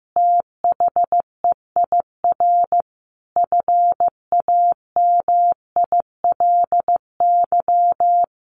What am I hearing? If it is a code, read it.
Morse code, heard as a keyed tone, about 15 wpm, 700 Hz. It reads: THEIR FAMILY
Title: Sentences from Top 1200 Words 15wpm